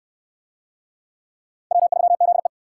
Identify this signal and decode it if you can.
Morse code, heard as a keyed tone, about 58 wpm, 700 Hz. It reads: W3ZE